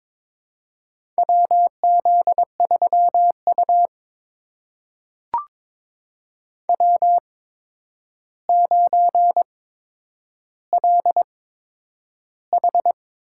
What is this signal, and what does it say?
Morse code, heard as a keyed tone, about 22 wpm, spaced as for 9 wpm, 700 Hz. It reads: WZ3U W9LH
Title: USA Call Signs - 4 Characters - 8x Character Spacing - 22wpm